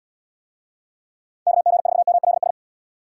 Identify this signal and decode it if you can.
Morse code, heard as a keyed tone, about 75 wpm, 700 Hz. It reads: KG4MLS